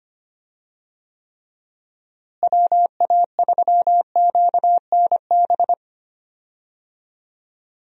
Morse code, heard as a keyed tone, about 25 wpm, 700 Hz. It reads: WA3QNB
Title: USA Call Signs - 6 Characters - Repeated 2x - 25wpm